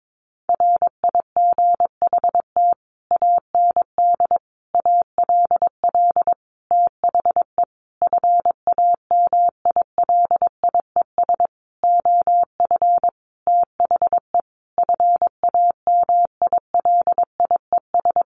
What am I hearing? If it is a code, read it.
Morse code, heard as a keyed tone, about 22 wpm, 700 Hz. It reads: RIGHT AND ALL THE FAMILIES OF THE FAMILIES
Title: A Sound Of Thunder 22wpm